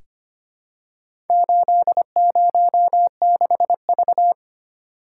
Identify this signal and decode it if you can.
Morse code, heard as a keyed tone, about 25 wpm, 690 Hz. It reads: 806V
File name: practice / 25wpm / Four Letter-Number Combinations - 25wpm